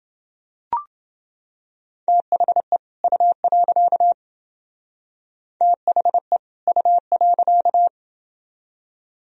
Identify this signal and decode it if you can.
Morse code, heard as a keyed tone, about 29 wpm, 700 Hz. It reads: THE U. THE U.